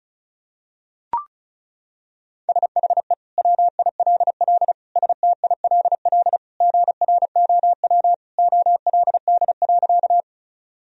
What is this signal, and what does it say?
Morse code, heard as a keyed tone, about 34 wpm, 700 Hz. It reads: SHE WILL STILL GROW OLD.
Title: Sentences from Top 1000 Words - Review Part 1 - 35wpm